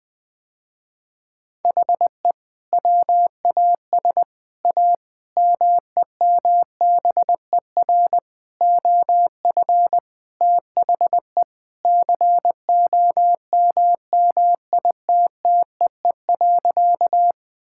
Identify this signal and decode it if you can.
Morse code, heard as a keyed tone, about 20 wpm, 700 Hz. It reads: HE WAS A MEMBER OF THE COMMITTEE.
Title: Sentences with 6-Word Phrases - Spoken Beginning - 20wpm